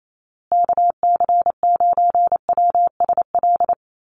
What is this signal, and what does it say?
Morse code, heard as a keyed tone, about 28 wpm, 700 Hz. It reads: KC9WSL
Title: USA Call Signs Repeated 2x 28wpm